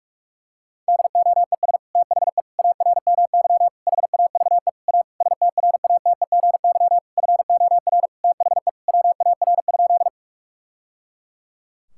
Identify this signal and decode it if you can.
Morse code, heard as a keyed tone, about 47 wpm, 700 Hz. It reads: DOES THE ARMY HAVE A STRATEGY FOR THE WAR?
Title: Sentences from Top 1000 Words - Review Part 5 - 45wpm